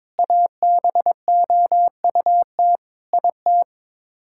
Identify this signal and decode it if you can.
Morse code, heard as a keyed tone, about 22 wpm, 700 Hz. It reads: ABOUT IT
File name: practice / 22wpm / Sets of 8 Words - 22wpm